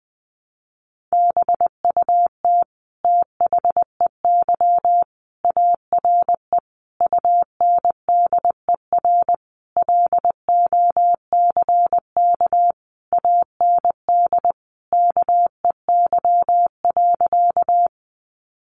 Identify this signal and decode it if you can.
Morse code, heard as a keyed tone, about 20 wpm, 700 Hz. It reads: BUT THEY ARE UNDER LOCK AND KEY.
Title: Sentences from Top 1300 Words 20wpm